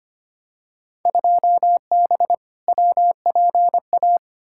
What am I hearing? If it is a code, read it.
Morse code, heard as a keyed tone, about 25 wpm, 700 Hz. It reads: 2B WPA
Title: Field Day Exchanges Encore - 25wpm